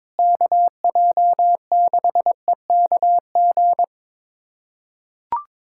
Morse code, heard as a keyed tone, about 21 wpm, 700 Hz. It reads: KJ6EKG
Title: USA Callsign DE Callsign - 6 Characters - 22wpm